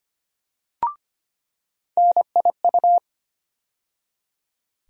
Morse code, heard as a keyed tone, about 25 wpm, 700 Hz. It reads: NIU
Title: USA Call Sign Postfixes - 3 Letters 25wpm